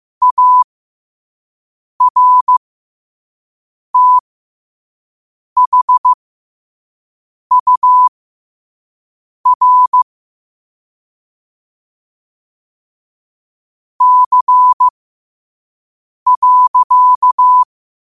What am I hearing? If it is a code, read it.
Morse code, heard as a keyed tone, about 15 wpm, spaced as for 5 wpm, 1000 Hz. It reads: ARTHUR C.
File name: qotdfiles / 5wpm